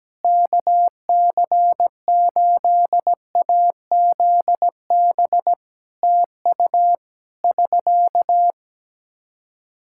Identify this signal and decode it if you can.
Morse code, heard as a keyed tone, about 17 wpm, 700 Hz. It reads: KC8AZB TU <SK>
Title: Practice QSO - Exchange 4 - Ending - 17wpm